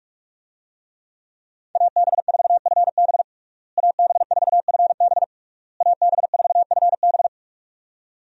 Morse code, heard as a keyed tone, about 46 wpm, 700 Hz. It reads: AB4FB AB4FB AB4FB
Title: USA Call Signs - 5 Characters - Repeated 3x - 45wpm